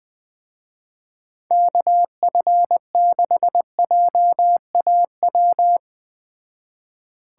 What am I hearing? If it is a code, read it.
Morse code, heard as a keyed tone, about 20 wpm, 700 Hz. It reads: KF6JAW